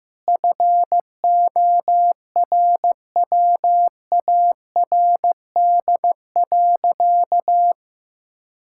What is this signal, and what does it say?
Morse code, heard as a keyed tone, about 15 wpm, 700 Hz. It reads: FORWARD.
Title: Pangram Challenge v2 - 15wpm